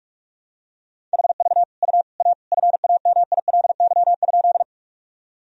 Morse code, heard as a keyed tone, about 41 wpm, 700 Hz. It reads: HV U A FAMILY?
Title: Sentences from Top 400 Words and Abbreviations 45wpm